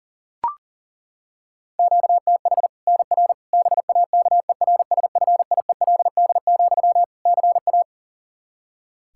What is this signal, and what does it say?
Morse code, heard as a keyed tone, about 40 wpm, 700 Hz. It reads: QTH NR BAKERSFIELD, CA
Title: K1USN SST QSO Practice 40wpm